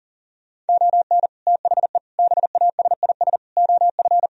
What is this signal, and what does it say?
Morse code, heard as a keyed tone, about 54 wpm, spaced as for 47 wpm, 700 Hz. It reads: ON THE BASIS OF
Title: Speed-Racing - Sets of 4 Words 40wpm